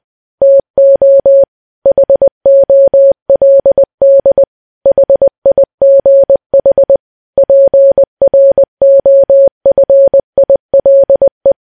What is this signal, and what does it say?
Morse code, heard as a keyed tone, about 20 wpm, 550 Hz. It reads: TO HOLD HIGH PROFILE
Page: news-headlines-in-morse-code-at-20-wpm